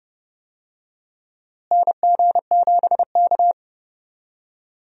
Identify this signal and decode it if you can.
Morse code, heard as a keyed tone, about 30 wpm, 700 Hz. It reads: NG7K
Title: USA Call Signs - 4 Characters - Repeated 2x - 30wpm